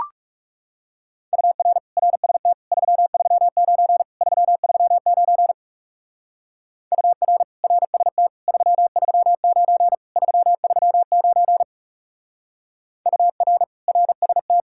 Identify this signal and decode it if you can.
Morse code, heard as a keyed tone, about 35 wpm, 700 Hz. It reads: UR RST 339 339 UR RST 339 339 UR RST
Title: Speed-Racing Common QSO Phrases - 30wpm